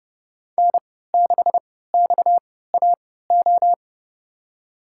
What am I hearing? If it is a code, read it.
Morse code, heard as a keyed tone, about 30 wpm, spaced as for 17 wpm, 700 Hz. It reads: N6XAO
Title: USA Call Signs - 3x Character Spacing - 30wpm